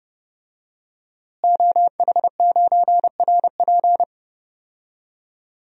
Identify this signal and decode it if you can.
Morse code, heard as a keyed tone, about 30 wpm, 700 Hz. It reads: OH9RP
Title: DXCC-100 Call Signs - 5 Characters - Repeated 2x - 30wpm